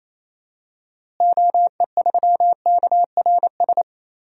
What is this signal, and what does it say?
Morse code, heard as a keyed tone, about 27 wpm, 700 Hz. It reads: OE3KRS
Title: DXCC-100 Call Signs - 6 Characters - Repeated 2x - 28wpm